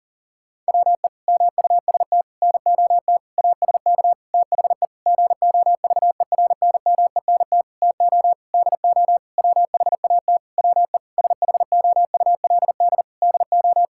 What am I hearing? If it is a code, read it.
Morse code, heard as a keyed tone, about 40 wpm, 700 Hz. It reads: WE MUST NOT ASK THE GOVERNMENT TO DO WHAT WE SHOULD DO
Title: Sentences from Top 1600 Words 40wpm